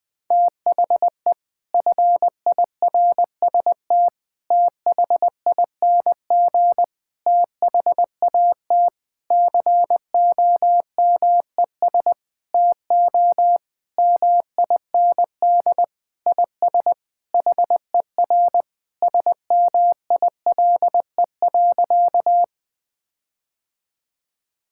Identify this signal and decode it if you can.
Morse code, heard as a keyed tone, about 19 wpm, 700 Hz. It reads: THE FIRST THING THAT COMES TO MIND IS HER SMILE.
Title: Sentences with 8-Word Phrases - Spoken Beginning - 20wpm